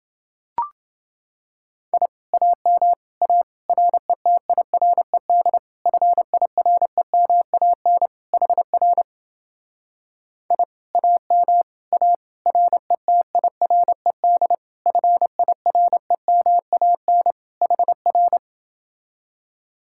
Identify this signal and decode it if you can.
Morse code, heard as a keyed tone, about 29 wpm, 700 Hz. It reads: I AM A RETIRED FIREMAN HR I AM A RETIRED FIREMAN HR